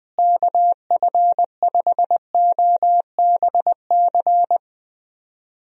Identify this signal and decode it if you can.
Morse code, heard as a keyed tone, about 20 wpm, 700 Hz. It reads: KF5OBC
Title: USA Call Signs - 6 Characters - Repeated 2x - 20wpm